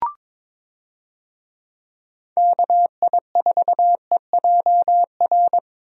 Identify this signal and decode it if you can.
Morse code, heard as a keyed tone, about 22 wpm, 700 Hz. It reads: KI4EJR
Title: USA Call Signs - 6 Characters - Repeated 3x - 22wpm